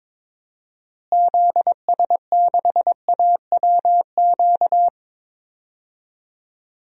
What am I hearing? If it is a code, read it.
Morse code, heard as a keyed tone, about 22 wpm, 700 Hz. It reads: ZS6AWQ
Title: DXCC-100 Call Signs - 6 Characters - Repeated 3x - 22wpm